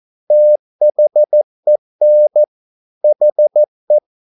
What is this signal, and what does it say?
Morse code, heard as a keyed tone, about 14 wpm, 600 Hz. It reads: THEN HE
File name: mp3 files / a dogs mistake - 14wpm